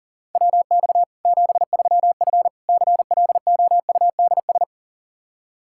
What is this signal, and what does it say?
Morse code, heard as a keyed tone, about 40 wpm, 700 Hz. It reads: WX 73F CLOUDS
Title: Practice QSO - Exchange 2 - Info - 40wpm